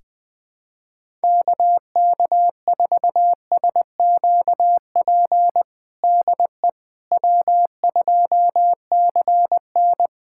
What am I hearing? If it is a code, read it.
Morse code, heard as a keyed tone, about 20 wpm, 700 Hz. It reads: KK4SQP DE W2CN
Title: USA Callsign DE Callsign 20wpm